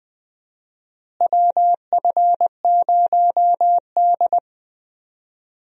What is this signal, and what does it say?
Morse code, heard as a keyed tone, about 20 wpm, 700 Hz. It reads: WF0D
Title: USA Call Signs - 4 Characters - Repeated 3x - 20wpm